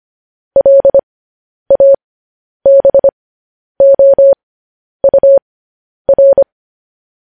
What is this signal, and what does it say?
Morse code, heard as a keyed tone, about 25 wpm, spaced as for 10 wpm, 550 Hz. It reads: LABOUR
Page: News Headlines in Morse Code at 10 WPM